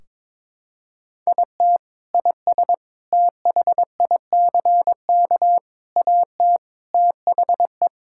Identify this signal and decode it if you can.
Morse code, heard as a keyed tone, about 22 wpm, 700 Hz. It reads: IT IS THICK AT THE
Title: Sentences from Top 1600 Words 22wpm